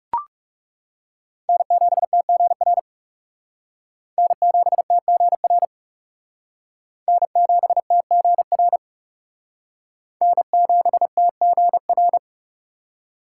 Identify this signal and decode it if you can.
Morse code, heard as a keyed tone, about 37 wpm, 700 Hz. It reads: N7TGR N7TGR N7TGR N7TGR
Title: Speed Racing - USA 1x3 Call Signs 30wpm